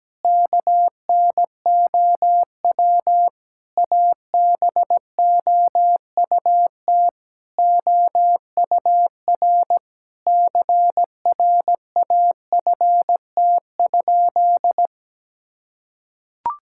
Morse code, heard as a keyed tone, about 17 wpm, 700 Hz. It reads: KNOW ABOUT OUR CRAFT?
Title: Sentences from Top 1000 Words - Review Part 6 - 17wpm